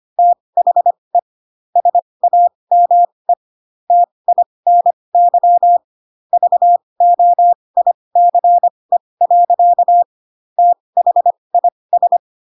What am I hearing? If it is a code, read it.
Morse code, heard as a keyed tone, about 25 wpm, spaced as for 20 wpm, 700 Hz. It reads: THE SAME TINY VOICE. THIS
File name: Pinocchio / Pinocchio ch1 15 25 ik1hge